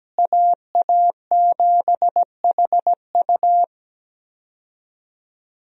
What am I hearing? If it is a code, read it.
Morse code, heard as a keyed tone, about 17 wpm, 700 Hz. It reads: AA7HU
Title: USA Call Signs - 5 Characters - Repeated 2x - 17wpm